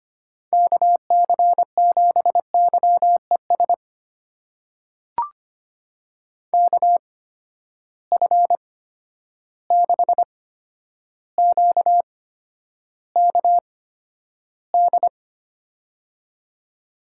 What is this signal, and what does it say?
Morse code, heard as a keyed tone, about 25 wpm, spaced as for 11 wpm, 700 Hz. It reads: KC7YES KF6QKD
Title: USA Call Signs - 6 Characters - 8x Character Spacing - 25wpm